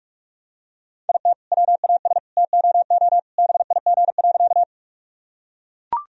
Morse code, heard as a keyed tone, about 44 wpm, 700 Hz. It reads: IT WAS TOO BIG.
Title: Sentences from Top 1000 Words - Review Part 1 - 45wpm